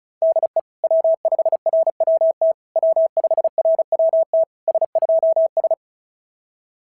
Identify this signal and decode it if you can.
Morse code, heard as a keyed tone, about 36 wpm, 650 Hz. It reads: DE W5RWT W5RWT S2S
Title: SOTA QSO Practice 35wpm